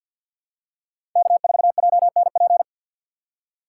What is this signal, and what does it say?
Morse code, heard as a keyed tone, about 53 wpm, 690 Hz. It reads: K4JNP
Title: USA Call Signs - 5 Characters - 3x Character Spacing - 50wpm